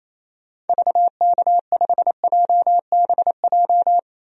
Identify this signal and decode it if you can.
Morse code, heard as a keyed tone, about 28 wpm, 700 Hz. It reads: VK5JBJ
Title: DXCC-100 Call Signs - 6 Characters - 28wpm